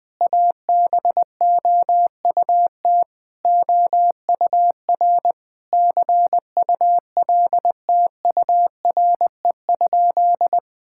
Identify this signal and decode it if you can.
Morse code, heard as a keyed tone, about 20 wpm, 700 Hz. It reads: ABOUT OUR CULTURE?
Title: Sentences from Top 1000 Words - Review Part 6 - 20wpm